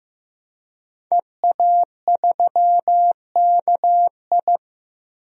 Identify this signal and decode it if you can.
Morse code, heard as a keyed tone, about 15 wpm, 700 Hz. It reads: EA3KI
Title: DXCC-100 Call Signs - 5 Characters - 15wpm